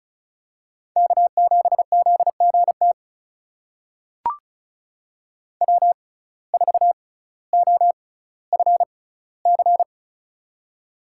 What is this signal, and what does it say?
Morse code, heard as a keyed tone, about 35 wpm, spaced as for 18 wpm, 700 Hz. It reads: K7ZGT W4OFC
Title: USA Call Signs - 5 Characters - 6x Character Spacing - 35wpm